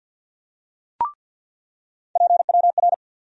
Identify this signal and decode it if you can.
Morse code, heard as a keyed tone, about 53 wpm, 700 Hz. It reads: PWR
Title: USA Call Sign Postfixes - 3 Letters 50wpm